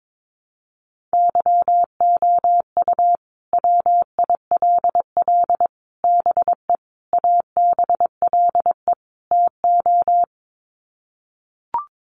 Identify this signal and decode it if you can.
Morse code, heard as a keyed tone, about 22 wpm, 700 Hz. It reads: YOU WILL BE ABLE TO
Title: Sets of 5 Words Encore2 22wpm